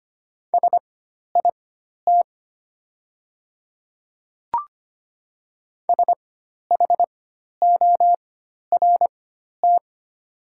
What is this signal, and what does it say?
Morse code, heard as a keyed tone, about 25 wpm, spaced as for 12 wpm, 690 Hz. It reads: SIT SHORT
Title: 4x Character Spacing - Top 500 Words - 25wpm